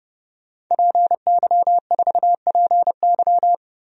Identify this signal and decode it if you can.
Morse code, heard as a keyed tone, about 30 wpm, 700 Hz. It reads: PY4PY
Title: DXCC-100 Call Signs - 5 Characters - 30wpm